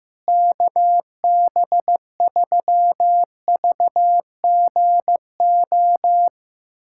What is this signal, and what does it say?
Morse code, heard as a keyed tone, about 15 wpm, 700 Hz. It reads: KB3VGO
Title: USA Callsign DE Callsign - 6 Characters - 15wpm